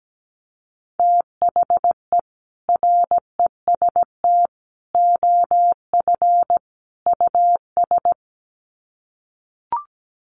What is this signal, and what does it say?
Morse code, heard as a keyed tone, about 17 wpm, 700 Hz. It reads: THE REST OF US
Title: Sets of 4 Words 17wpm